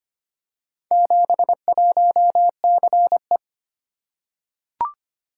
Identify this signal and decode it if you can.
Morse code, heard as a keyed tone, about 25 wpm, 700 Hz. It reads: 71CE
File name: practice / 25wpm / Four Letter-Number Combinations - 25wpm